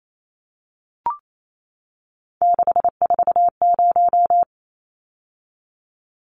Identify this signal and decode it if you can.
Morse code, heard as a keyed tone, about 28 wpm, 700 Hz. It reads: 640
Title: Numbers - 3-Digit 28wpm